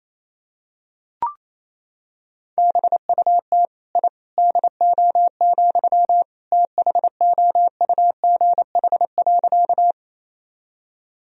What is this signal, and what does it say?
Morse code, heard as a keyed tone, about 28 wpm, 700 Hz. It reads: BUT I DO, THOUGH.
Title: Sentences from Top 1000 Words - Review Part 2 - 28wpm